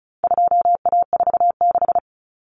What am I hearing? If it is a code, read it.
Morse code, heard as a keyed tone, about 34 wpm, 700 Hz. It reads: 2A46
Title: Common Cut Numbers - 4-Digit 35wpm